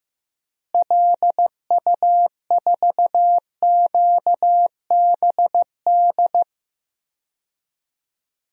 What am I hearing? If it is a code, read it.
Morse code, heard as a keyed tone, about 15 wpm, 700 Hz. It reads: LU4QBD
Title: DXCC-100 Call Signs - 6 Characters - Repeated 3x - 15wpm